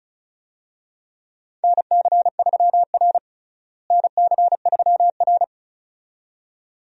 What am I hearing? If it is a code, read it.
Morse code, heard as a keyed tone, about 34 wpm, 700 Hz. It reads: NC3R NC3R
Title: USA Call Signs - 4 Characters - Repeated 2x - 35wpm